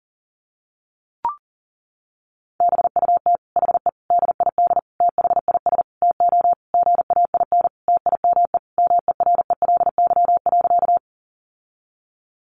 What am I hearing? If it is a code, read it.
Morse code, heard as a keyed tone, about 37 wpm, 700 Hz. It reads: BUT HE DID THIS TO GAIN TIME MERELY.
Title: Sentences from Top 1400 Words 40wpm